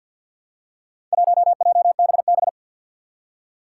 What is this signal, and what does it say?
Morse code, heard as a keyed tone, about 48 wpm, 700 Hz. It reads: 1JBB
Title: Four Challenging Letter-Number Combinations 50wpm